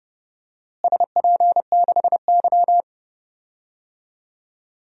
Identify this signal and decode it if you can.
Morse code, heard as a keyed tone, about 30 wpm, 700 Hz. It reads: SP6Y
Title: DXCC-100 Call Signs - 4 Characters - Repeated 2x - 30wpm